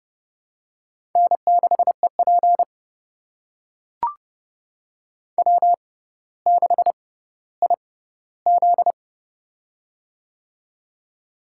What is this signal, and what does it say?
Morse code, heard as a keyed tone, about 32 wpm, spaced as for 16 wpm, 700 Hz. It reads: N6EP W6IZ
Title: USA Call Signs - 4 Characters - 6x Character Spacing - 30wpm